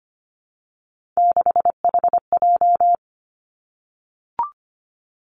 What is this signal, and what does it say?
Morse code, heard as a keyed tone, about 25 wpm, 700 Hz. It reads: 6HJ
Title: Three Challenging Letter-Number Combinations 25wpm